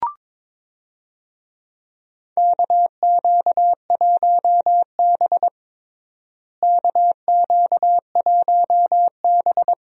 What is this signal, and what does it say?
Morse code, heard as a keyed tone, about 22 wpm, 700 Hz. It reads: KQ1B KQ1B
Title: USA Call Signs - 4 Characters - Repeated 2x - 22wpm